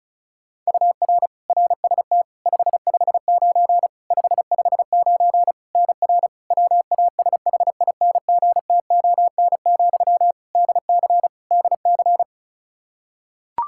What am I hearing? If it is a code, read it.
Morse code, heard as a keyed tone, about 36 wpm, 700 Hz. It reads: UR RST 559 559 NR WASHINGTON, DC DC
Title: Practice QSO - Exchange 1 - Beginning - 35wpm